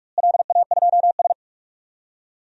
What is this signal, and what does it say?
Morse code, heard as a keyed tone, about 46 wpm, 700 Hz. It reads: LA2S
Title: DXCC-100 Call Signs - 4 Characters - Repeated 3x - 45wpm